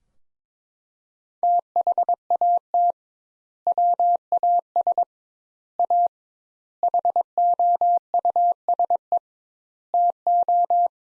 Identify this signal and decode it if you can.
Morse code, heard as a keyed tone, about 21 wpm, 700 Hz. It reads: THAT WAS A HOUSE TO
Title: Sentences Top 200 Words - 22wpm